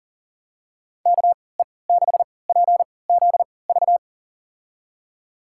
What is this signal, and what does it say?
Morse code, heard as a keyed tone, about 40 wpm, spaced as for 23 wpm, 700 Hz. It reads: KE6PZV
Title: USA Call Signs - 6 Characters - 3x Character Spacing - 40wpm